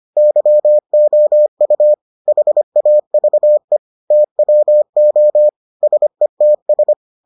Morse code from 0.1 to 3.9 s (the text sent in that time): YOU HAVE